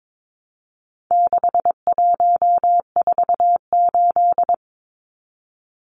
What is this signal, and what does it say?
Morse code, heard as a keyed tone, about 22 wpm, 700 Hz. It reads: 6148
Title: Numbers - 4-Digit 22wpm